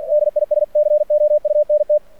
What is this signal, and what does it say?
Morse code, heard as a keyed tone, about 51 wpm, 600 Hz. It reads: DISCOUNT
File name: archived / CRSnet